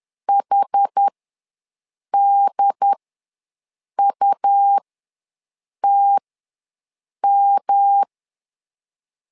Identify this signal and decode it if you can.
Morse code, heard as a keyed tone, about 10 wpm, spaced as for 6 wpm, 800 Hz. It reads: HDUTM
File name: LEZIONI / 3 MWRO / 3 MWR ws 10wpm 2151